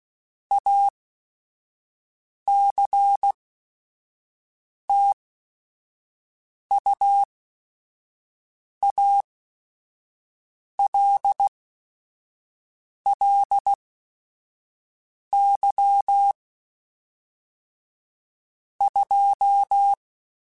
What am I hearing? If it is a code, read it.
Morse code, heard as a keyed tone, about 16 wpm, spaced as for 5 wpm, 800 Hz. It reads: ACTUALLY 2
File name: Technical Subjects / Technical Subjects-mp3s / Sw Pwr Sup @5